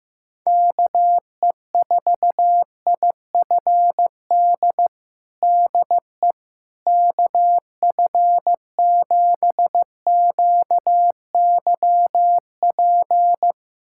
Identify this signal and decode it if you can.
Morse code, heard as a keyed tone, about 15 wpm, 700 Hz. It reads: KE4IFD DE KF7QYP
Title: USA Callsign DE Callsign - 6 Characters - 15wpm